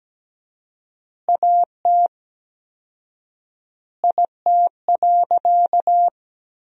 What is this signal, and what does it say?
Morse code, heard as a keyed tone, about 17 wpm, 700 Hz. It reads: AT IT.
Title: 4x Word-Spacing - Sentences from Top 300 Words - 17wpm